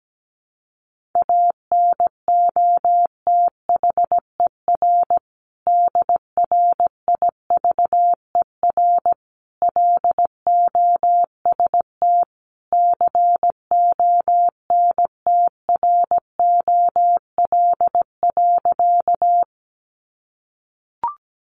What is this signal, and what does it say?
Morse code, heard as a keyed tone, about 17 wpm, 700 Hz. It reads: ANOTHER DRIVER LOST CONTROL.
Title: Sentences from Top 1400 Words 17wpm